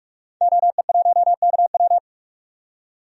Morse code, heard as a keyed tone, about 45 wpm, 700 Hz. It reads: OE1KW
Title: DXCC-100 Call Signs - 5 Characters - 45wpm